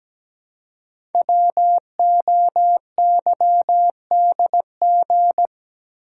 Morse code, heard as a keyed tone, about 17 wpm, 700 Hz. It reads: WOYDG
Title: Code Groups - 5 Letters - 17wpm